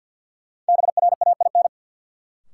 Morse code, heard as a keyed tone, about 49 wpm, 700 Hz. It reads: BRAIN
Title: Sentences from Top 1000 Words - Review Part 6 - New Word Drill - 50wpm